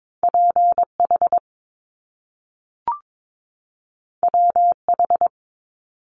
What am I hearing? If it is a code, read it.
Morse code, heard as a keyed tone, about 22 wpm, 700 Hz. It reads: PH WH
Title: Top Two Letter Combinations in Words 22wpm